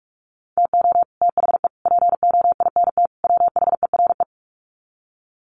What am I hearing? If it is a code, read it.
Morse code, heard as a keyed tone, about 44 wpm, 700 Hz. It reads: TO THE POINT WHERE
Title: Sets of 4 Words 45wpm